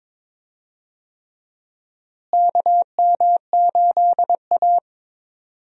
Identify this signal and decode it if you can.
Morse code, heard as a keyed tone, about 22 wpm, 700 Hz. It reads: KM8A
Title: USA Call Signs - 4 Characters - Repeated 3x - 22wpm